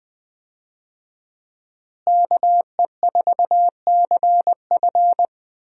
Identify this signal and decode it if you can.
Morse code, heard as a keyed tone, about 20 wpm, 700 Hz. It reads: KE4CF
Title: USA Call Signs - 5 Characters - Repeated 2x - 20wpm